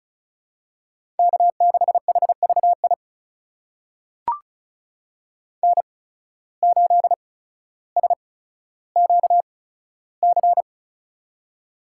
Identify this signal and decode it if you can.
Morse code, heard as a keyed tone, about 34 wpm, spaced as for 15 wpm, 700 Hz. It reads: K6HVI N8SQC
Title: USA Call Signs - 5 Characters - 8x Character Spacing - 35wpm